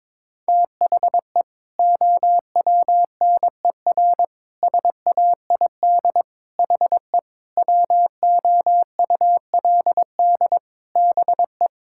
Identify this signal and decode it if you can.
Morse code, heard as a keyed tone, about 22 wpm, 700 Hz. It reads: THE OWNER SAID HE WOULD BE
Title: Sentences from Top 1600 Words 22wpm